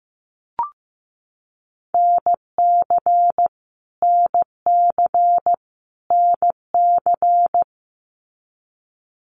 Morse code, heard as a keyed tone, about 15 wpm, 700 Hz. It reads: NC NC NC
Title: US States Repeated 3x 15wpm